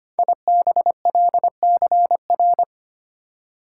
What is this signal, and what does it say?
Morse code, heard as a keyed tone, about 25 wpm, 690 Hz. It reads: IBLCR
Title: Code Groups - 5 Letter:Numbers - 25wpm